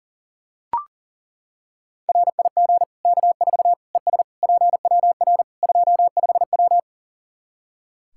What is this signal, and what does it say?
Morse code, heard as a keyed tone, about 37 wpm, 700 Hz. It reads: RIG K4 ES PWR 25W
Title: Practice QSO - Exchange 2 - Info - 40wpm